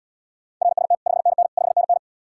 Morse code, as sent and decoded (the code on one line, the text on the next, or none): .-. ... - / ..... -. -. / ..... -. -.
RST 5NN 5NN